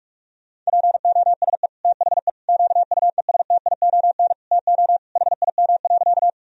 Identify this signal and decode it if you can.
Morse code, heard as a keyed tone, about 41 wpm, 700 Hz. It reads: POSE THE QUESTION TO HIM.
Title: Sentences from Top 1000 Words - Review Part 2 - 45wpm